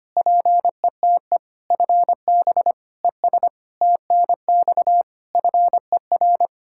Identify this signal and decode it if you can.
Morse code, heard as a keyed tone, about 25 wpm, 700 Hz. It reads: PETE FB ES TNX FER
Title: Practice QSO - Exchange 2 - Info - 25wpm